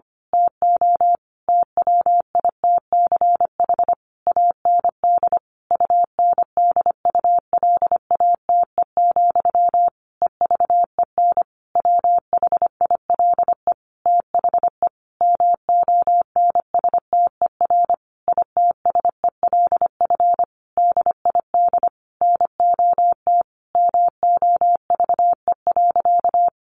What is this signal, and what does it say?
Morse code, heard as a keyed tone, about 25 wpm, 700 Hz. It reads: TO TWITCH AND UNDULATE, EVEN WHILE THE MONSTER ITSELF DID NOT MOVE.
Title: A Sound Of Thunder 25wpm